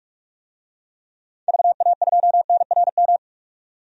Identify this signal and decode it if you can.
Morse code, heard as a keyed tone, about 46 wpm, 700 Hz. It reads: VA2NRM
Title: DXCC-100 Call Signs - 6 Characters - 45wpm